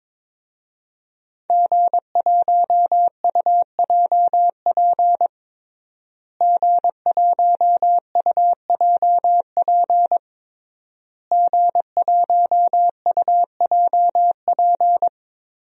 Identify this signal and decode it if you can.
Morse code, heard as a keyed tone, about 22 wpm, 700 Hz. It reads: G1UJP G1UJP G1UJP
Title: DXCC-100 Call Signs - 5 Characters - Repeated 3x - 22wpm